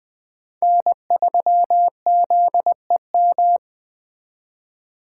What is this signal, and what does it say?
Morse code, heard as a keyed tone, about 20 wpm, 700 Hz. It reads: N3ZEM
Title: Speed Racing - USA 1x3 Call Signs 17wpm